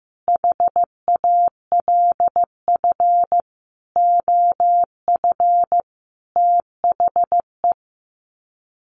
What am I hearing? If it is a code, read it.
Morse code, heard as a keyed tone, about 15 wpm, 700 Hz. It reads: HALF OF THE
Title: Sets of 5 Words 15wpm